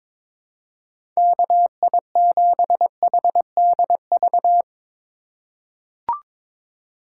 Morse code, heard as a keyed tone, about 21 wpm, 700 Hz. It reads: KI7HDV
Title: USA Call Signs - 6 Characters - 2x Character Spacing - 22wpm